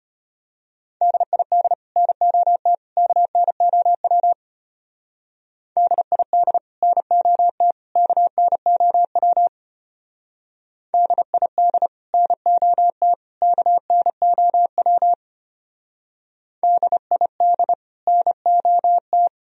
Morse code, heard as a keyed tone, about 32 wpm, 700 Hz. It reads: DID NOT KNOW DID NOT KNOW DID NOT KNOW DID NOT